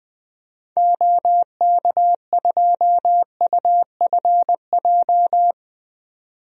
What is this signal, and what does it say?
Morse code, heard as a keyed tone, about 21 wpm, 700 Hz. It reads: OK2UFJ
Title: DXCC-100 Call Signs - 6 Characters - 20wpm